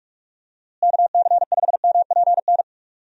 Morse code, heard as a keyed tone, about 44 wpm, 700 Hz. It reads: KC5MPN
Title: USA Call Signs - 6 Characters - Repeated 2x - 45wpm